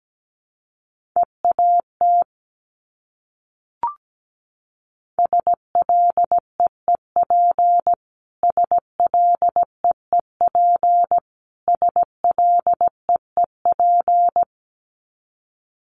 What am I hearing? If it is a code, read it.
Morse code, heard as a keyed tone, about 17 wpm, 700 Hz. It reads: EAT SLEEP SLEEP SLEEP
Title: Top 500 Words Repeated 17wpm